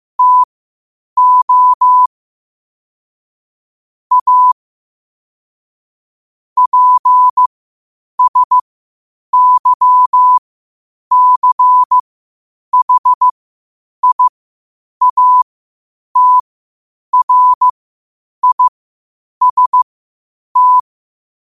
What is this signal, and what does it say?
Morse code, heard as a keyed tone, about 15 wpm, spaced as for 8 wpm, 1000 Hz. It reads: TO A PSYCHIATRIST
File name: qotdfiles / 8wpm